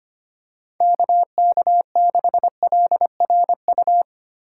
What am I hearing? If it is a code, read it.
Morse code, heard as a keyed tone, about 25 wpm, 700 Hz. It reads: KK6LRU
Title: USA Call Signs - 6 Characters - 3x Character Spacing - 25wpm